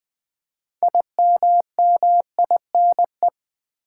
Morse code, heard as a keyed tone, about 20 wpm, 700 Hz. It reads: IMMINE
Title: Build A Word - 3 Letters At A Time - 8-Letter Words - 20wpm